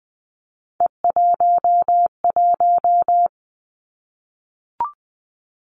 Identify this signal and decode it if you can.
Morse code, heard as a keyed tone, about 20 wpm, 700 Hz. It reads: E11
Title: Three Challenging Letter-Number Combinations 20wpm